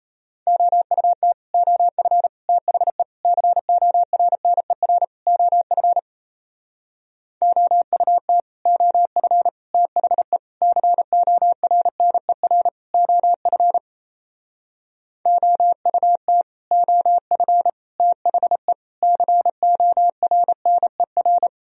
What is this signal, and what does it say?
Morse code, heard as a keyed tone, about 35 wpm, 700 Hz. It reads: OUT OF THE CORNER OF OUT OF THE CORNER OF OUT OF THE CORNER
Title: Speed-Racing - Sets of 5 Words 28wpm